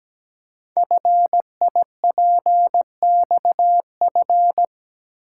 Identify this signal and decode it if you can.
Morse code, heard as a keyed tone, about 17 wpm, 700 Hz. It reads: FIPXF
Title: Code Groups - 5 Letters - 17wpm